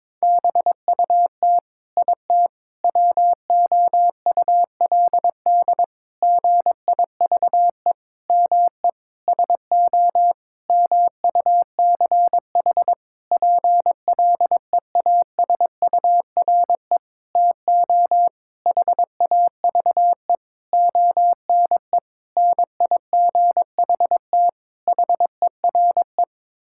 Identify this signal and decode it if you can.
Morse code, heard as a keyed tone, about 21 wpm, 700 Hz. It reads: BUT IT WOULD GIVE ME SO MUCH PLEASURE TO HAVE ONE NIGHT HERE
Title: Sentences from Top 1600 Words 22wpm